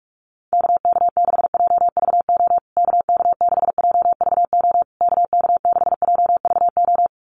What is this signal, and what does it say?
Morse code, heard as a keyed tone, about 44 wpm, 700 Hz. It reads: KK6JVO KK6JVO KK6JVO
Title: USA Call Signs Repeated 3x 45wpm